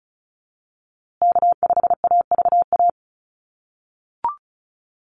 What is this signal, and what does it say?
Morse code, heard as a keyed tone, about 34 wpm, 700 Hz. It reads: K5AVA
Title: USA Call Signs Repeated 3x 35wpm